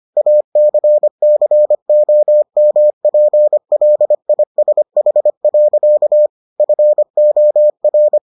Morse code, heard as a keyed tone, about 25 wpm, 600 Hz. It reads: ACCOMPLISH. FOR